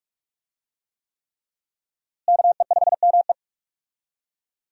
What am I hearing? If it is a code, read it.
Morse code, heard as a keyed tone, about 46 wpm, 700 Hz. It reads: KE5ME
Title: USA Call Signs - 5 Characters - Repeated 3x - 45wpm